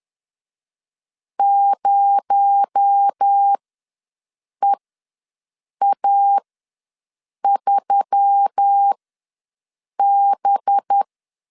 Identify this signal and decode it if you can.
Morse code, heard as a keyed tone, about 10 wpm, spaced as for 6 wpm, 800 Hz. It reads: 0EA3B
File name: LEZIONI / 6 PKX0 / 6 PKX0 ws 10wpm 4220